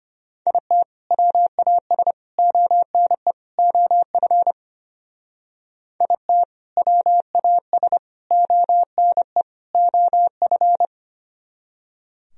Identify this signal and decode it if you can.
Morse code, heard as a keyed tone, about 29 wpm, 700 Hz. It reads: IT WAS ONE OF IT WAS ONE OF
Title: Speed-Racing - Sets of 4 Words 25wpm